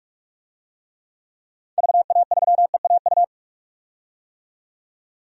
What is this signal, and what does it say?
Morse code, heard as a keyed tone, about 43 wpm, 700 Hz. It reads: VA3EAU
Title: DXCC-100 Call Signs - 6 Characters - 45wpm